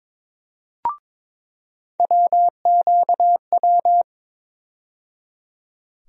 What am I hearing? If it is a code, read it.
Morse code, heard as a keyed tone, about 22 wpm, 700 Hz. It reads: WQW
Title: USA Call Sign Postfixes - 3 Letters 22wpm